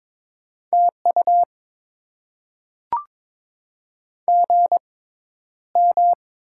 Morse code, heard as a keyed tone, about 22 wpm, spaced as for 11 wpm, 700 Hz. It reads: TU GM
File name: Top 20 QSO Elements - 6x Character Spacing - 22wpm